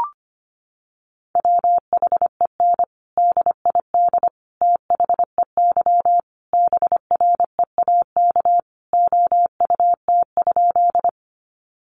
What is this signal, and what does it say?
Morse code, heard as a keyed tone, about 25 wpm, 700 Hz. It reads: WHEN DID THEY BREAK OUT?
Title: Sentences from Top 900 Words 25wpm